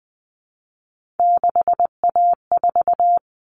Morse code, heard as a keyed tone, about 20 wpm, 700 Hz. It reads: 6A4
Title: Common Cut Numbers - 3-Digit 20wpm